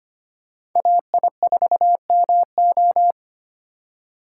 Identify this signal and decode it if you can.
Morse code, heard as a keyed tone, about 25 wpm, 700 Hz. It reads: AI4MO
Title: USA Call Signs - 6x Character Spacing - 25wpm